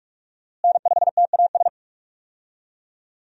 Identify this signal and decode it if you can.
Morse code, heard as a keyed tone, about 46 wpm, 700 Hz. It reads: N5TAS
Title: USA Call Signs - 5 Characters - Repeated 3x - 45wpm